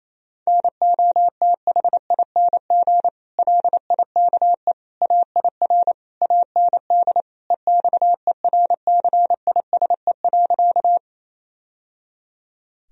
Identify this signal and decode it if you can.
Morse code, heard as a keyed tone, about 28 wpm, 700 Hz. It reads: NOTHING LIKE AIR AND EXERCISE.
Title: Sentences from Top 1000 Words - Review Part 4 - 28wpm